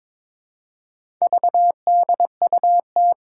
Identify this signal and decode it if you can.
Morse code, heard as a keyed tone, about 22 wpm, 700 Hz. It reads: VDUT
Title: Four Challenging Letter-Number Combinations 22wpm